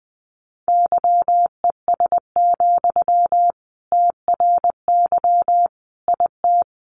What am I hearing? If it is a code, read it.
Morse code, heard as a keyed tone, about 20 wpm, 700 Hz. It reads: YES, TRY IT
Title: Sentences from Top 600 Words 20wpm